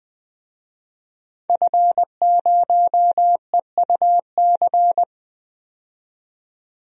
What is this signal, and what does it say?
Morse code, heard as a keyed tone, about 20 wpm, 700 Hz. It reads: F0EUC
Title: DXCC-100 Call Signs - 5 Characters - 20wpm